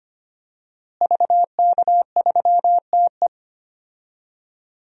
Morse code, heard as a keyed tone, about 25 wpm, 700 Hz. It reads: VK3TE
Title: DXCC-100 Call Signs - 5 Characters - Repeated 3x - 25wpm